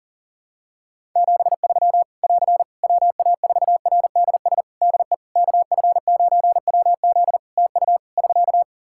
Morse code, heard as a keyed tone, about 37 wpm, 700 Hz. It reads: 73 <AR> WA4RDS DE KF9WZ TU <SK>
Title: K1USN SST QSO Practice 40wpm